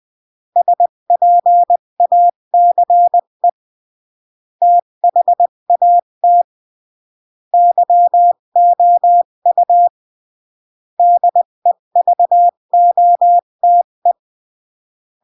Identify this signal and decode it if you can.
Morse code, heard as a keyed tone, about 20 wpm, 700 Hz. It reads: SPACE THAT YOU DEVOTE
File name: CW / jodorowsky / 82 commandments 20-15